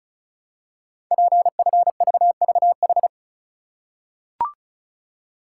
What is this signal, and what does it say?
Morse code, heard as a keyed tone, about 36 wpm, 700 Hz. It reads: PFVVH
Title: Code Groups - 5 Letters - 35wpm